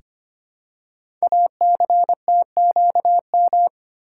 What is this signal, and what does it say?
Morse code, heard as a keyed tone, about 25 wpm, 700 Hz. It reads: ACTQM
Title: Code Groups - 5 Letters - 25wpm